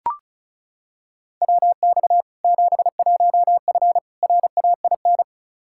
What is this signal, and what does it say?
Morse code, heard as a keyed tone, about 34 wpm, 700 Hz. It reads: WX 71F RAIN
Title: K1USN SST QSO Practice 35wpm